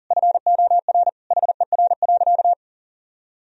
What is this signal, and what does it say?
Morse code, heard as a keyed tone, about 40 wpm, 700 Hz. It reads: FOR HER.
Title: Sentences from Top 1000 Words - Review Part 3 - 40wpm